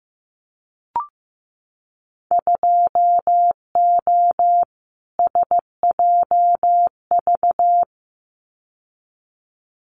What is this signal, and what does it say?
Morse code, heard as a keyed tone, about 15 wpm, 700 Hz. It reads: 2O SJV
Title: Winter Field Day Exchanges Encore 15wpm